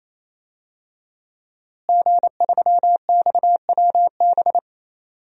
Morse code, heard as a keyed tone, about 27 wpm, 700 Hz. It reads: G3XWB
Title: DXCC-100 Call Signs - 5 Characters - Repeated 3x - 28wpm